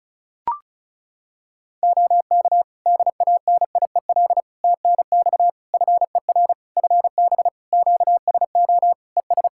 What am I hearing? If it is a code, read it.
Morse code, heard as a keyed tone, about 36 wpm, 700 Hz. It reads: OK DANIEL TNX FER FB QSO ES
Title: K1USN SST QSO Practice 35wpm